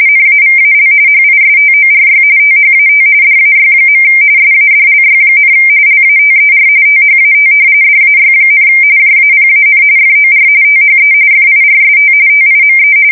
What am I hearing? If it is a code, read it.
RTTY, about 45.45 baud. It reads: NAVAL HISTORY THAT DAY. SADLY, LATER THAT DAY, JAPAN FOR THE FIRST TIME
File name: fsk170 / battleshipnj